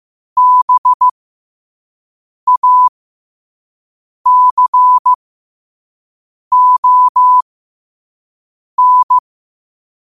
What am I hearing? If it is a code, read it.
Morse code, heard as a keyed tone, about 15 wpm, spaced as for 5 wpm, 1000 Hz. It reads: BACON